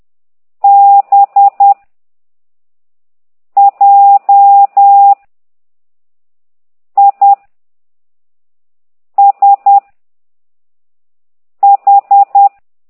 Morse code, heard as a keyed tone, about 10 wpm, 800 Hz. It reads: B J I S H
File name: exerc7